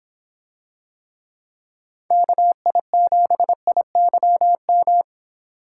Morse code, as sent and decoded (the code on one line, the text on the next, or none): -.- .. --... .. -.-- --
KI7IYM